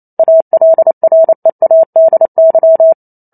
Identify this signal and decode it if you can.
Morse code, heard as a keyed tone, about 28 wpm, 650 Hz. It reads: ALREADY